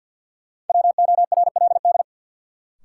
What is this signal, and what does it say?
Morse code, heard as a keyed tone, about 47 wpm, 700 Hz. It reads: WORLD
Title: Sets of 2 Words - New Word Drill - 50wpm